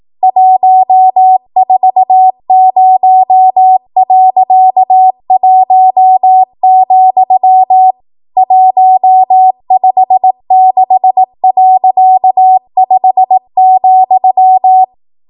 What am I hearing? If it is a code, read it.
Morse code, heard as a keyed tone, about 18 wpm, 750 Hz. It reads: 140.1, 156.5,
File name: Archive / 18 WPM / 18WPM